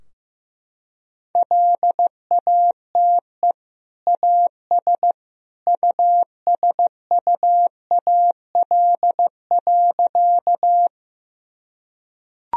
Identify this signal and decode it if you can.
Morse code, heard as a keyed tone, about 16 wpm, 700 Hz. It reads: LATE AS USUAL.
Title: Sentences from Top 1000 Words - Review Part 2 - 15wpm